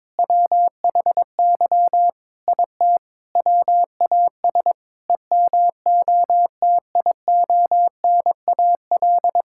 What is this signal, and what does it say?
Morse code, heard as a keyed tone, about 22 wpm, 700 Hz. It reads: WHY IT WAS EMOTIONAL
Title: Sentences from Top 1000 Words - Review Part 5 - 22wpm